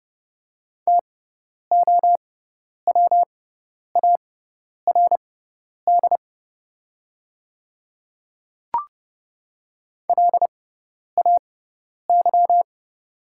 Morse code, heard as a keyed tone, about 31 wpm, spaced as for 10 wpm, 700 Hz. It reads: TOWARD LAY
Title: 6x Character Spacing - Top 500 Words - 30wpm